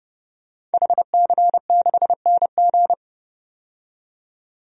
Morse code, heard as a keyed tone, about 32 wpm, 700 Hz. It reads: HC6NG
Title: DXCC-100 Call Signs - 5 Characters - 30wpm